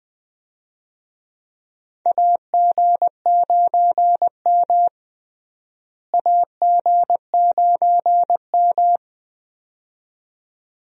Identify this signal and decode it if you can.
Morse code, heard as a keyed tone, about 20 wpm, 700 Hz. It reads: AG9M AG9M
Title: USA Call Signs - 4 Characters - Repeated 2x - 20wpm